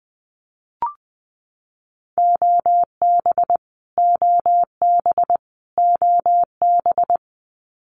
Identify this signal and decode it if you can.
Morse code, heard as a keyed tone, about 20 wpm, 700 Hz. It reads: OB OB OB
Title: Abbreviations Repeated 3x - 20wpm